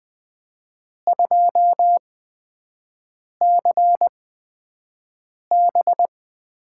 Morse code, heard as a keyed tone, about 20 wpm, spaced as for 5 wpm, 700 Hz. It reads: 2CB
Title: USA Call Signs - 8x Character Spacing - 20wpm